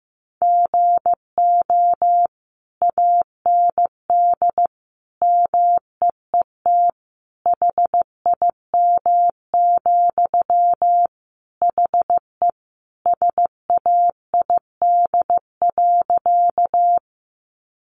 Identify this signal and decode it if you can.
Morse code, heard as a keyed tone, about 15 wpm, 700 Hz. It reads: GO AND MEET HIM, HE SAID.
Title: Sentences from Top 1100 Words 15wpm